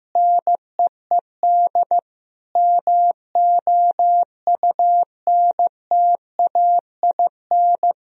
Morse code, heard as a keyed tone, about 15 wpm, 700 Hz. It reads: NEED MOUNTAIN
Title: Sentences from Top 1000 Words - Review Part 1 - 15wpm